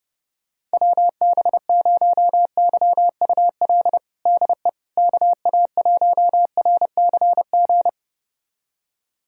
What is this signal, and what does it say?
Morse code, heard as a keyed tone, about 30 wpm, 700 Hz. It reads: WB0YUL DE KA1RCG
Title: USA Callsign DE Callsign 30wpm